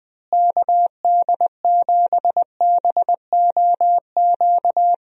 Morse code, heard as a keyed tone, about 21 wpm, 700 Hz. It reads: KD7BOQ
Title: USA Call Signs - 6 Characters - Repeated 3x - 20wpm